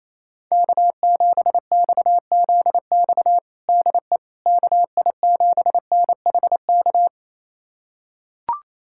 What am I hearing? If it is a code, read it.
Morse code, heard as a keyed tone, about 28 wpm, 700 Hz. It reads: K7XZX DE KI7NHK
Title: USA Callsign DE Callsign 28wpm